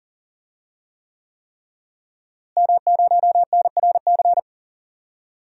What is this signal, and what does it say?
Morse code, heard as a keyed tone, about 39 wpm, 700 Hz. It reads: M0NRC
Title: DXCC-100 Call Signs - 5 Characters - Repeated 3x - 40wpm